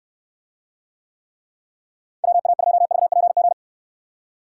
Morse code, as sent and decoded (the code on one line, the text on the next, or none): .-- .- ..--- ...- .--. --.
WA2VPG